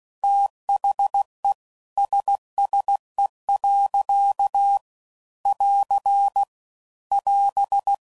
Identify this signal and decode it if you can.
Morse code, heard as a keyed tone, about 16 wpm, 800 Hz. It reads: THE SSE. <AR> <AS>
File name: CW Code Groups @15WPM/Weather Sets